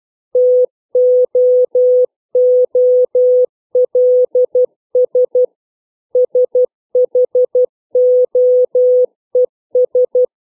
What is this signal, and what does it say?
Morse code, heard as a keyed tone, about 12 wpm, 500 Hz. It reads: TOOLS SHOES
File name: CW / fixed length words